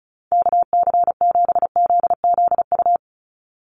Morse code, heard as a keyed tone, about 36 wpm, 700 Hz. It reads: KC7ZZU
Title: USA Call Signs 35wpm